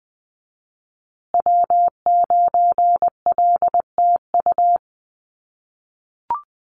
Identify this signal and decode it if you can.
Morse code, heard as a keyed tone, about 20 wpm, 700 Hz. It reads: W9LTU
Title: USA Call Signs Repeated 3x 20wpm